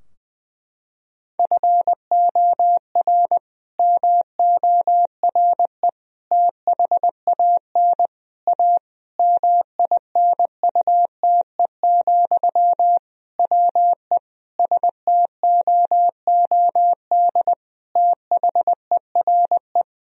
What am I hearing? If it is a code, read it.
Morse code, heard as a keyed tone, about 20 wpm, 700 Hz. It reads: FOR MORE THAN A MINUTE, WE STOOD THERE
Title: Sentences from Top 1000 Words - Review Part 2 - 20wpm